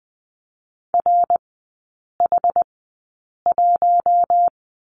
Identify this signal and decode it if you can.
Morse code, heard as a keyed tone, about 20 wpm, spaced as for 8 wpm, 700 Hz. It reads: RH1
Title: 114 Lesson 19e - Letter w - Three Characters 20wpm